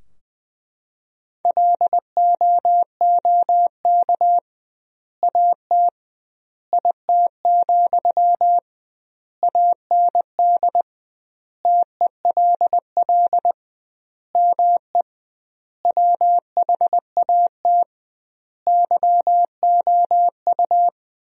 Morse code, heard as a keyed tone, about 20 wpm, 700 Hz. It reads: LOOK AT IT, AND TELL ME WHAT YOU
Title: 2x Word-Spacing - Sentences from Top 200 Words - 20wpm